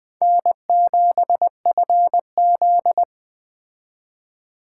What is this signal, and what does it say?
Morse code, heard as a keyed tone, about 20 wpm, 700 Hz. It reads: N7FZ